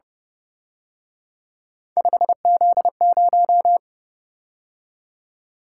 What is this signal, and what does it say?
Morse code, heard as a keyed tone, about 27 wpm, 700 Hz. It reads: 5Z0
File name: Speed-Racing - Three Letter-Number Combinations - 22wpm